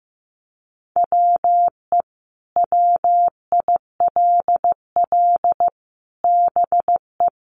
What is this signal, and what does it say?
Morse code, heard as a keyed tone, about 15 wpm, 700 Hz. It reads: WE WILL BE
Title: Sets of 5 Words Encore 15wpm